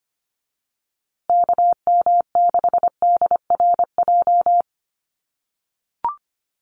Morse code, heard as a keyed tone, about 25 wpm, 700 Hz. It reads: KM6DRJ
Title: USA Call Signs Repeated 3x 25wpm